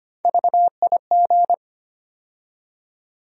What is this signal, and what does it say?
Morse code, heard as a keyed tone, about 25 wpm, 700 Hz. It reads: VIG
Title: USA Call Sign Postfixes - 3 Letters 25wpm